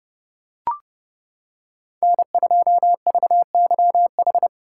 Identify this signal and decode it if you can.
Morse code, heard as a keyed tone, about 30 wpm, 700 Hz. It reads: N2VYH